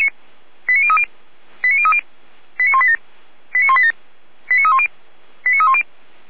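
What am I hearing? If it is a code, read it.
Five-tone selective call (ZVEI), heard as quick runs of five tones, tones of about 70 ms. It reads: 89030 89030 89187 89187 89310 89310
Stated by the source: ZVEI five-tone selective call received with an SDR.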